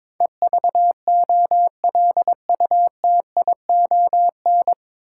Morse code, heard as a keyed tone, about 21 wpm, 700 Hz. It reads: EVOLUTION
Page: Morse Code Ninja practice 22wpm